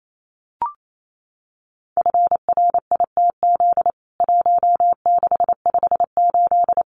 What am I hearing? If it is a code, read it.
Morse code, heard as a keyed tone, about 28 wpm, 700 Hz. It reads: FRITZ 1658
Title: CWT Exchange 28wpm